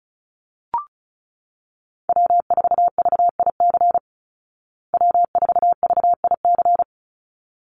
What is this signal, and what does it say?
Morse code, heard as a keyed tone, about 34 wpm, 700 Hz. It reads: W4VIC W4VIC
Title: Call Signs Repeated 3x - 35wpm